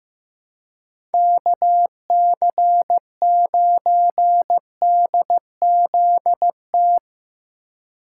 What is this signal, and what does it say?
Morse code, heard as a keyed tone, about 16 wpm, 700 Hz. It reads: KC9DZT
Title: Speed Racing - USA 2x3 Call Signs 15wpm